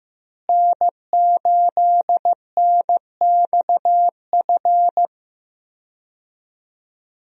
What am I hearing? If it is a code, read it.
Morse code, heard as a keyed tone, about 16 wpm, 700 Hz. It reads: N8NXF
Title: USA Call Signs - 5 Characters - Repeated 2x - 15wpm